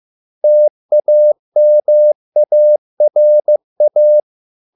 Morse code, heard as a keyed tone, about 15 wpm, 600 Hz. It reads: TAMARA